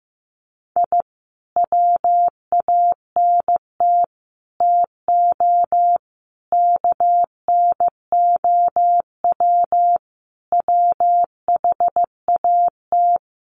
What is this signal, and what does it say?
Morse code, heard as a keyed tone, about 15 wpm, 700 Hz. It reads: I WANT TO KNOW WHAT
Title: Sets of 5 Words Encore2 15wpm